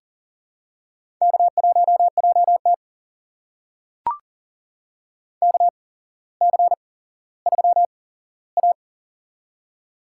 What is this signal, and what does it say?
Morse code, heard as a keyed tone, about 40 wpm, spaced as for 17 wpm, 700 Hz. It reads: K1JT KC3A
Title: USA Call Signs - 4 Characters - 8x Character Spacing - 40wpm